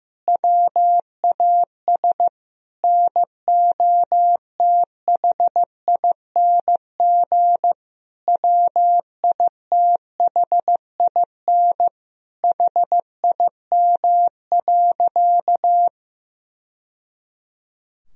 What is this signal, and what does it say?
Morse code, heard as a keyed tone, about 15 wpm, 700 Hz. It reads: WAS NOTHING WITHIN HIM.
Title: Sentences from Top 1000 Words - Review Part 4 - 15wpm